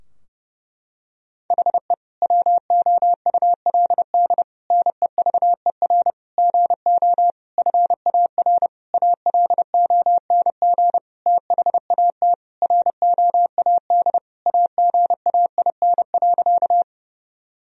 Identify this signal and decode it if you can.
Morse code, heard as a keyed tone, about 29 wpm, 700 Hz. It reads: HE WOULD NEVER GO FAR ALONG THAT ROAD AGAIN.
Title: Sentences from Top 1000 Words - Review Part 2 - 30wpm